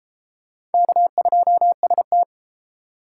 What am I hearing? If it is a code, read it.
Morse code, heard as a keyed tone, about 28 wpm, 700 Hz. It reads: K2ST